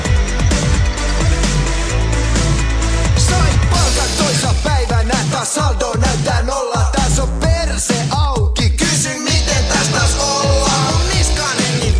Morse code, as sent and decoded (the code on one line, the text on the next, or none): none